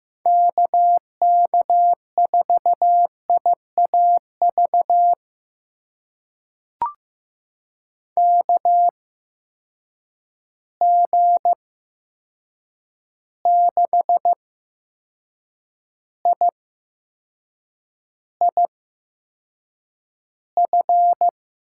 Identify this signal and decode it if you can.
Morse code, heard as a keyed tone, about 15 wpm, spaced as for 6 wpm, 700 Hz. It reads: KK4IAV KG6IIF
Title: USA Call Signs - 6 Characters - 8x Character Spacing - 15wpm